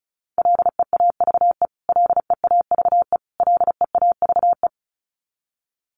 Morse code, heard as a keyed tone, about 34 wpm, 700 Hz. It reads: LEAVE LEAVE LEAVE
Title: Top 500 Words Repeated 35wpm